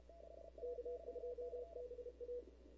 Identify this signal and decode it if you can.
Morse code, heard as a keyed tone, about 30 wpm, 490 Hz, with noise. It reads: K2BA
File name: CW / 2005-k5tr arrl cw dx -19:00